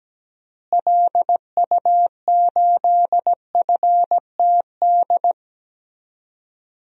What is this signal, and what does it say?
Morse code, heard as a keyed tone, about 17 wpm, 700 Hz. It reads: LU8FTD
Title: DXCC-100 Call Signs - 6 Characters - Repeated 2x - 17wpm